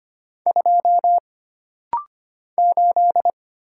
Morse code, heard as a keyed tone, about 25 wpm, 700 Hz. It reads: 2 8
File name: Single Character (Letter:Number:Punctuation:Prosign) - Rapid-Fire - 25wpm